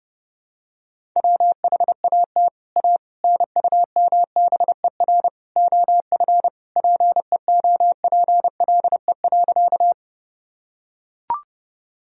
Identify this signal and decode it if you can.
Morse code, heard as a keyed tone, about 30 wpm, 700 Hz. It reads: WHAT A NUMBER OF PEOPLE.
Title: Sentences from Top 1000 Words - Review Part 1 - 30wpm